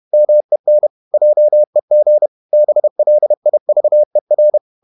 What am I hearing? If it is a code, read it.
Morse code, heard as a keyed tone, about 31 wpm, 600 Hz. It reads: MEN JEG BLIVER
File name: Ferie QRQ 31wpm